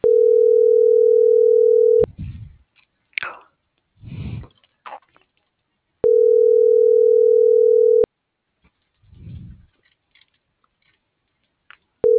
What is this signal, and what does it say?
Ringback tone (North America), 440 + 480 Hz.